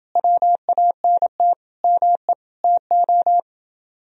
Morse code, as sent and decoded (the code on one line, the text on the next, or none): .-- .- -. - / -- . / - ---
WANT ME TO